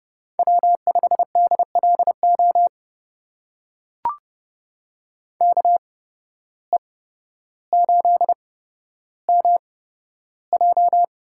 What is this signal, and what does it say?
Morse code, heard as a keyed tone, about 30 wpm, spaced as for 13 wpm, 700 Hz. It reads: W5DLO KE8MJ
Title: USA Call Signs - 8x Character Spacing - 30wpm